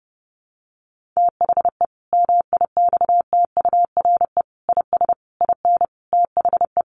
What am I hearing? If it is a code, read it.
Morse code, heard as a keyed tone, about 30 wpm, 700 Hz. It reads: THE MIXTURE IS IN THE
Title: Sentences from Top 1500 Words 30wpm